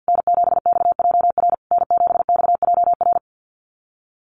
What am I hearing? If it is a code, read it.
Morse code, heard as a keyed tone, about 50 wpm, 700 Hz. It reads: N7XJR N7XJR
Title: USA Call Signs Repeated 3x 50wpm